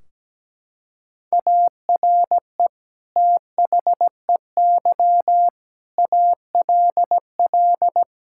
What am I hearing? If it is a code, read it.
Morse code, heard as a keyed tone, about 17 wpm, 700 Hz. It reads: ARE THEY ALL
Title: Sentences from Top 1600 Words 17wpm